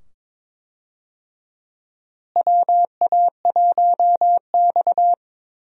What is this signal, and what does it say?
Morse code, heard as a keyed tone, about 22 wpm, 700 Hz. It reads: WA1X
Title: USA Call Signs - 4 Characters - Repeated 2x - 22wpm